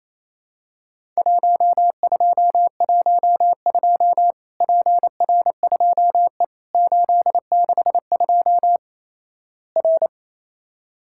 Morse code, heard as a keyed tone, about 28 wpm, 700 Hz. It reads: 1212 PR2E 862 R
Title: Worked All Europe DX Contest - QTC Exchange - 28wpm